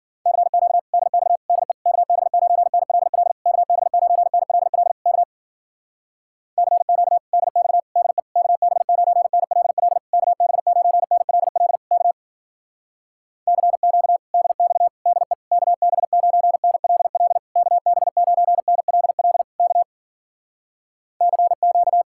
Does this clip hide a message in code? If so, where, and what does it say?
Yes: Morse code from 0.3 to 22.0 s, heard as a keyed tone, about 53 wpm, 690 Hz. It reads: CQ DX DE KB9NLL KB9NLL K CQ DX DE KB9NLL KB9NLL K CQ DX DE KB9NLL KB9NLL K CQ